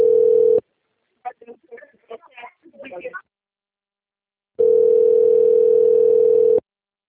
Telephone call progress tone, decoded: ringback tone

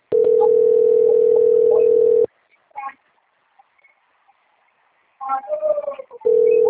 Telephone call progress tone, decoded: ringback tone

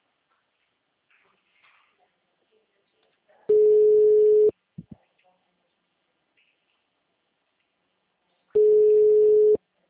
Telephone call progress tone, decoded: ringback tone

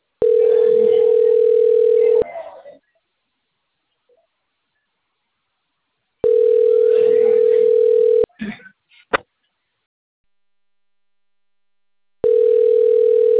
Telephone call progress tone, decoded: ringback tone

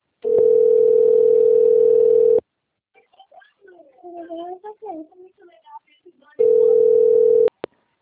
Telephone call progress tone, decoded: ringback tone

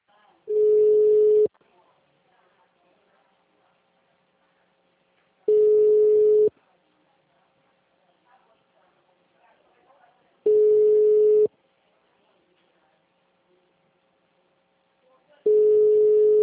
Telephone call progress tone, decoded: ringback tone

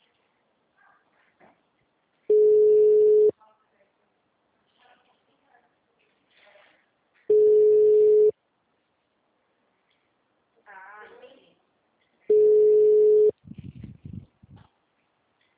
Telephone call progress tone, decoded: ringback tone

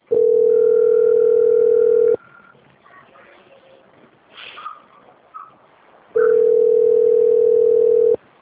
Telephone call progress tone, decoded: ringback tone

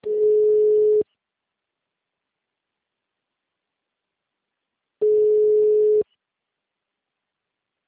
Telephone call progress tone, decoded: ringback tone

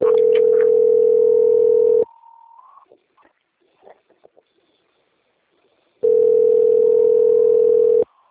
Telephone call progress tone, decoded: ringback tone

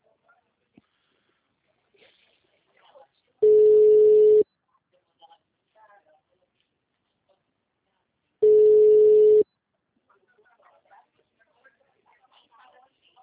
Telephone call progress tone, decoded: ringback tone

